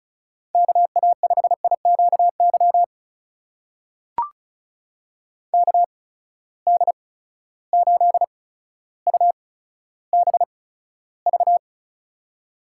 Morse code, heard as a keyed tone, about 36 wpm, spaced as for 15 wpm, 700 Hz. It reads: KA5IQY KD8UBV
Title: USA Call Signs - 6 Characters - 8x Character Spacing - 35wpm